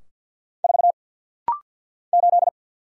Morse code, heard as a keyed tone, about 49 wpm, 700 Hz. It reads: 4 8
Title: Single Letter-Number - Rapid-Fire 50wpm